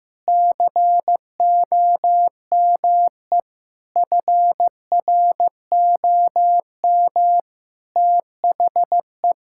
Morse code, heard as a keyed tone, about 15 wpm, 700 Hz. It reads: COME FROM THE
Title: Sentences from Top 1000 Words - Review Part 5 - 15wpm